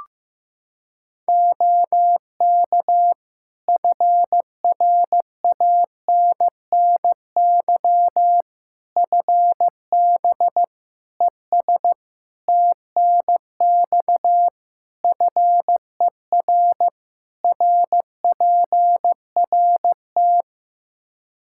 Morse code, heard as a keyed tone, about 16 wpm, 700 Hz. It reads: OK FRANNY FB ES TNX FER RPRT
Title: Practice QSO - Exchange 2 - Info - 15wpm